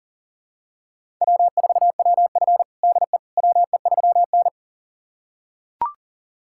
Morse code, heard as a keyed tone, about 40 wpm, 700 Hz. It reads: W4WF DE WE3N
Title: USA Callsign DE Callsign - 4 Characters - 40wpm